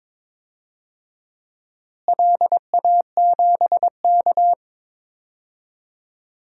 Morse code, heard as a keyed tone, about 22 wpm, 700 Hz. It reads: LA7K
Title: DXCC-100 Call Signs - 4 Characters - Repeated 2x - 22wpm